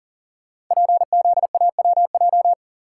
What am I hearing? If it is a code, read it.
Morse code, heard as a keyed tone, about 40 wpm, 700 Hz. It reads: PZAWJ
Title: Code Groups - 5 Letter:Numbers - 40wpm